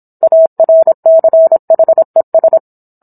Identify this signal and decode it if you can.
Morse code, heard as a keyed tone, about 27 wpm, 650 Hz. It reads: ARCHES